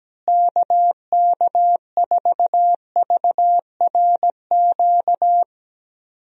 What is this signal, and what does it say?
Morse code, heard as a keyed tone, about 17 wpm, 700 Hz. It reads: KK4VRQ
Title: USA Call Signs - 6 Characters - Repeated 3x - 17wpm